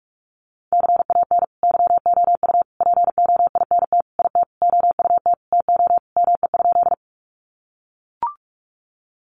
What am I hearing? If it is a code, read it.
Morse code, heard as a keyed tone, about 44 wpm, 700 Hz. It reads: CAN YOU POINT IT OUT TO ME?
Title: Sentences from Top 200 Words 45wpm